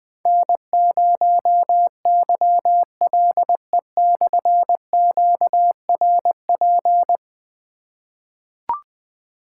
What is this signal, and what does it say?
Morse code, heard as a keyed tone, about 20 wpm, 700 Hz. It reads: N0YLE/QRP
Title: Difficult Call Signs 20wpm